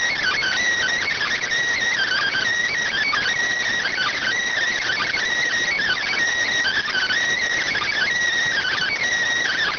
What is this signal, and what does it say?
Emergency Alert System (SAME) tones (header bursts), heard as harsh buzzing bursts. It is heard from 1.1 to 8.8 s.